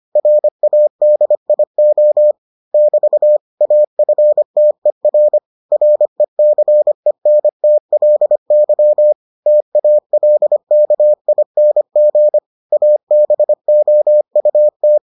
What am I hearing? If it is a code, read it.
Morse code, heard as a keyed tone, about 25 wpm, 600 Hz. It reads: RADIO <BT> AFTER RECENTLY TALKING ABOUT